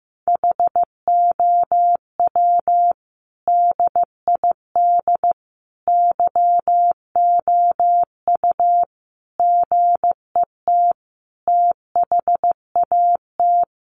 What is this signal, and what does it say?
Morse code, heard as a keyed tone, about 15 wpm, 700 Hz. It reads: HOW DID YOU GET THAT
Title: Sentences from Top 900 Words 15wpm